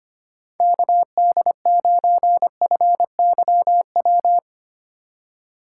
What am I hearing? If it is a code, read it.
Morse code, heard as a keyed tone, about 25 wpm, 700 Hz. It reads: KD9FYW
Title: USA Call Signs - 6 Characters - 3x Character Spacing - 25wpm